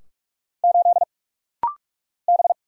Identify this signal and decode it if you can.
Morse code, heard as a keyed tone, about 44 wpm, 700 Hz. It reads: 8 B
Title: Single Letter-Number Alt - Rapid-Fire 45wpm